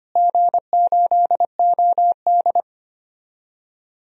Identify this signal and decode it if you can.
Morse code, heard as a keyed tone, about 25 wpm, 700 Hz. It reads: G8OD
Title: DXCC-100 Call Signs - 4 Characters - Repeated 3x - 25wpm